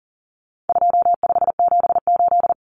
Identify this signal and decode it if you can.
Morse code, heard as a keyed tone, about 40 wpm, 700 Hz. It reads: 2578
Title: Common Cut Numbers - 4-Digit 40wpm